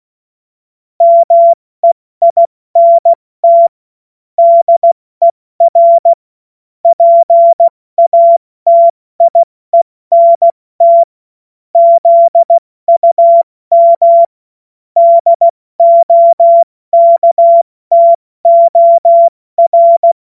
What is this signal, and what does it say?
Morse code, heard as a keyed tone, about 16 wpm, 680 Hz. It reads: MEINT DER PATIENT ZUM DOKTOR
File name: Morsekurs / Witze / Witz 11. Juli 2023 70 F80